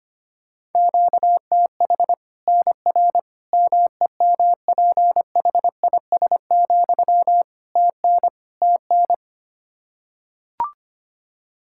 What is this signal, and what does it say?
Morse code, heard as a keyed tone, about 25 wpm, 700 Hz. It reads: QTH NR MEMPHIS, TN TN
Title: Practice QSO - Exchange 1 - Beginning - 25wpm